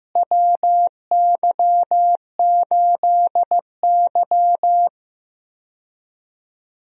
Morse code, heard as a keyed tone, about 15 wpm, 700 Hz. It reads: WY8Y
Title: USA Call Signs - 4 Characters - Repeated 2x - 15wpm